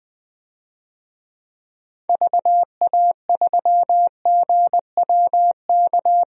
Morse code, heard as a keyed tone, about 20 wpm, 700 Hz. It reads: VA3GWK
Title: DXCC-100 Call Signs - 6 Characters - 20wpm